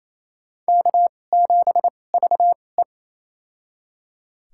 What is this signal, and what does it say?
Morse code, heard as a keyed tone, about 28 wpm, spaced as for 20 wpm, 700 Hz. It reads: K7VE
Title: USA Call Signs - 2x Character Spacing - 28wpm